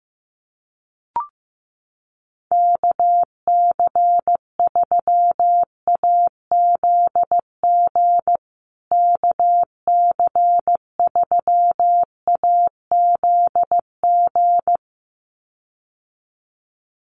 Morse code, heard as a keyed tone, about 15 wpm, 700 Hz. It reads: KC3AZG KC3AZG
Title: USA Call Signs Repeated 2x 15wpm